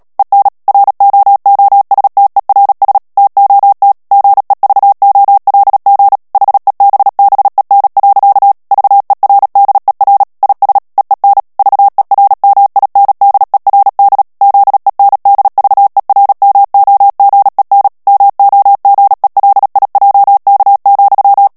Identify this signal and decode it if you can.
Morse code, heard as a keyed tone, about 37 wpm, 800 Hz. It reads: EN ROOSTERS TOT GEVOLG HEBBEN. VERDER IS EEN VERMINDERD ZENDVERMOGEN MOGELIJK,
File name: morse mp3 / HOLLANDAIS texte clair 2 vitesse 35 mots minute